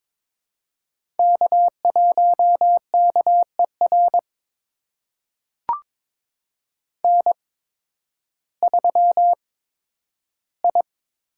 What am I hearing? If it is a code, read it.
Morse code, heard as a keyed tone, about 23 wpm, spaced as for 12 wpm, 700 Hz. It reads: K1KER N3I
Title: USA Call Signs - 5 Characters - 8x Character Spacing - 22wpm